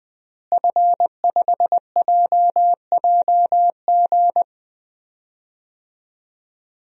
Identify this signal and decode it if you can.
Morse code, heard as a keyed tone, about 20 wpm, 700 Hz. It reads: F5JJG
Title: DXCC-100 Call Signs - 5 Characters - Repeated 2x - 20wpm